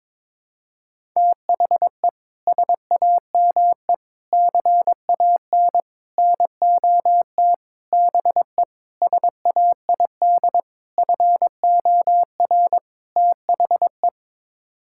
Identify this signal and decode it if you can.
Morse code, heard as a keyed tone, about 22 wpm, 700 Hz. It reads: THE SAME CAN NOT BE SAID FOR THE
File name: practice / 22wpm / Sets of 8 Words - 22wpm